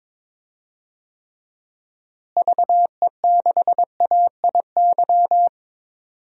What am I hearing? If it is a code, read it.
Morse code, heard as a keyed tone, about 22 wpm, 700 Hz. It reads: VE6AIY
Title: DXCC-100 Call Signs - 6 Characters - Repeated 3x - 22wpm